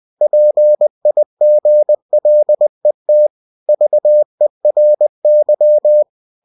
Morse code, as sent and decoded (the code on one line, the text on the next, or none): .--. .. --. .-.. - / ...- . .-. -.--
PIGLET VERY